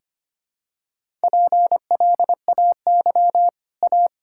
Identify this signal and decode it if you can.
Morse code, heard as a keyed tone, about 25 wpm, 700 Hz. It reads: PLAY A
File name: Easy Sentences with Hints - 25wpm